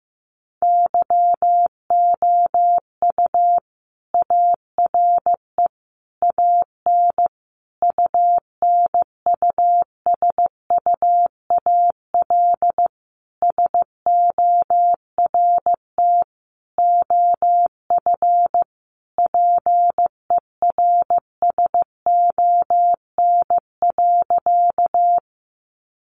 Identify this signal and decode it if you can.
Morse code, heard as a keyed tone, about 15 wpm, 700 Hz. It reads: YOU ARE AN UNUSUAL SORT OF PERSON.
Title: Sentences from Top 1400 Words 15wpm